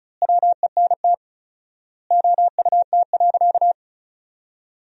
Morse code, heard as a keyed tone, about 36 wpm, 700 Hz. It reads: WENT OUT.
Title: 4x Word-Spacing - Sentences from Top 300 Words - 35wpm